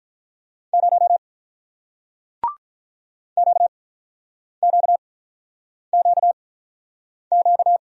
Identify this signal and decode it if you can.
Morse code, heard as a keyed tone, about 45 wpm, spaced as for 9 wpm, 700 Hz. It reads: 0 QQQQ